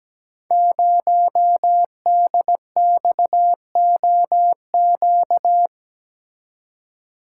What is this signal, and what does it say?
Morse code, heard as a keyed tone, about 17 wpm, 700 Hz. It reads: 0DXOQ
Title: Code Groups - 5 Letter:Numbers - 17wpm